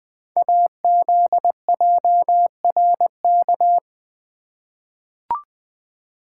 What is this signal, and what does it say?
Morse code, heard as a keyed tone, about 20 wpm, 700 Hz. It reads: AZJRK
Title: Code Groups - 5 Letter:Numbers - 20wpm